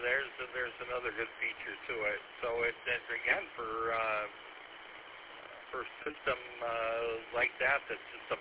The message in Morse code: none

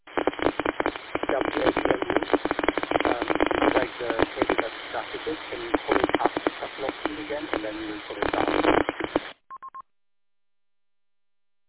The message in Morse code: none

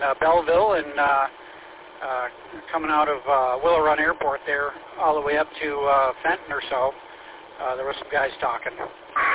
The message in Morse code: none

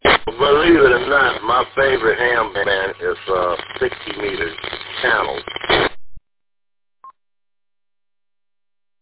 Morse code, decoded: none

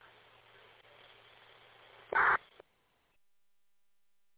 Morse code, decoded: none

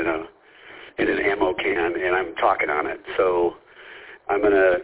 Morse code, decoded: none

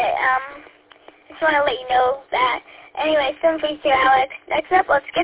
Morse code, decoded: none

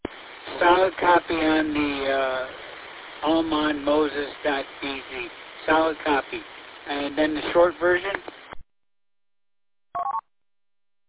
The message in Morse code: none

